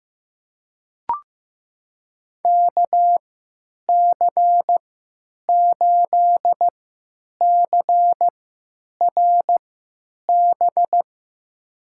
Morse code, heard as a keyed tone, about 16 wpm, spaced as for 9 wpm, 700 Hz. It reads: KC8CRB
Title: USA Call Signs - 6 Characters - 3x Character Spacing - 15wpm